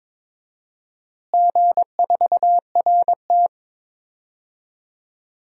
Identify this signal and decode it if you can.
Morse code, heard as a keyed tone, about 22 wpm, 700 Hz. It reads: G4RT
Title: DXCC-100 Call Signs - 4 Characters - Repeated 2x - 22wpm